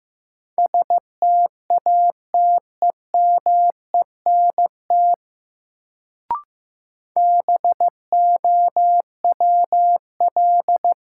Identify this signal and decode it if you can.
Morse code, heard as a keyed tone, about 15 wpm, 700 Hz. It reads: STATEMENT BOWL
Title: Sentences from Top 1200 Words - New Word Drill - 15wpm